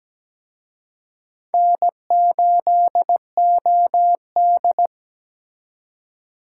Morse code, heard as a keyed tone, about 20 wpm, 700 Hz. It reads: N8OD
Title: Speed Racing - USA 1x2 Call Signs 17wpm